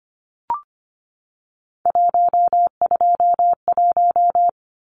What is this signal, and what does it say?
Morse code, heard as a keyed tone, about 25 wpm, 700 Hz. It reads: 121
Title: Numbers - 3-Digit 25wpm